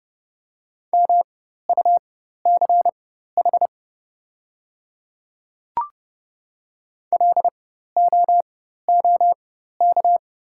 Morse code, heard as a keyed tone, about 30 wpm, spaced as for 14 wpm, 700 Hz. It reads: MUCH LOOK
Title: 4x Character Spacing - Top 200 Words - 30wpm